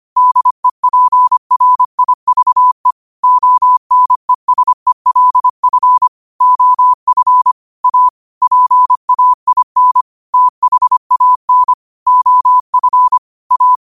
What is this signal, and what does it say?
Morse code, heard as a keyed tone, about 25 wpm, 1000 Hz. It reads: DEPRIVE ONESELF OF A PAIN THAN OF A